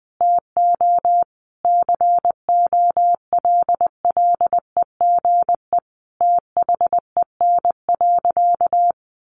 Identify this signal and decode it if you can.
Morse code, heard as a keyed tone, about 20 wpm, 700 Hz. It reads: TO COLLEGE THEN.
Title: Sentences from Top 1300 Words 20wpm